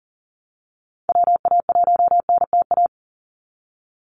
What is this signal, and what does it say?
Morse code, heard as a keyed tone, about 40 wpm, 700 Hz. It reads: WA1NTA
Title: Call Signs Repeated 2x - 40wpm